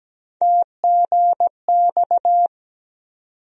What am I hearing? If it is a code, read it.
Morse code, heard as a keyed tone, about 17 wpm, 700 Hz. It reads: TGX
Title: USA Call Sign Postfixes - 3 Letters 17wpm